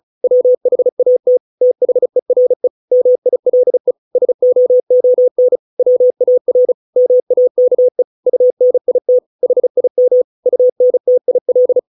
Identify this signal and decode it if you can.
Morse code, heard as a keyed tone, about 34 wpm, 500 Hz. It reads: WHAT THERE MILE SOON WAR MAKE UNIT HIM UNTIL